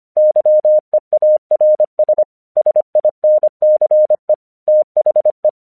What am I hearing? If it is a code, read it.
Morse code, heard as a keyed tone, about 25 wpm, 610 Hz. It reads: YEARS SINCE THE
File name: 1 English 25wpm / AQC05